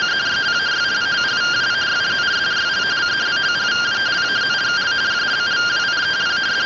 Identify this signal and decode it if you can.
Two-tone teleprinter signal, about 170 Hz shift.